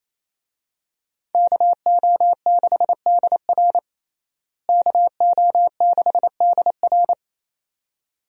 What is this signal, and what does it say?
Morse code, heard as a keyed tone, about 28 wpm, 700 Hz. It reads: KO6DR KO6DR
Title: USA Call Signs - 5 Characters - Repeated 2x - 28wpm